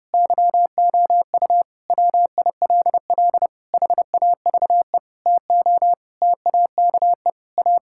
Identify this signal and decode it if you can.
Morse code, heard as a keyed tone, about 32 wpm, 700 Hz. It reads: YOU WILL HAVE TO TAKE A
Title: Sentences from Top 1000 Words - Review Part 6 - 30wpm